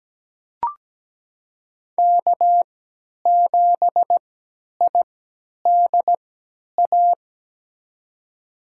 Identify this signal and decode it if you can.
Morse code, heard as a keyed tone, about 17 wpm, spaced as for 10 wpm, 700 Hz. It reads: K7IDA
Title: USA Call Signs - 3x Character Spacing - 17wpm